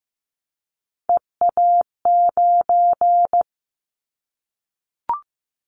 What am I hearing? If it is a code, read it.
Morse code, heard as a keyed tone, about 15 wpm, 700 Hz. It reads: EA9
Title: Callsign Prefixes for Top 100 DXCC - 15wpm